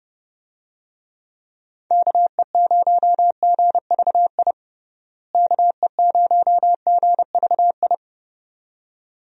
Morse code, heard as a keyed tone, about 30 wpm, 700 Hz. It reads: KE0GVI KE0GVI
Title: USA Call Signs - 6 Characters - Repeated 2x - 30wpm